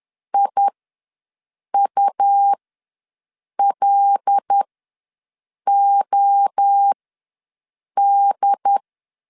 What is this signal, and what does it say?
Morse code, heard as a keyed tone, about 10 wpm, spaced as for 6 wpm, 800 Hz. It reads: IULOD
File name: LEZIONI / 5 LBVC / 5 L ws 10wpm 3636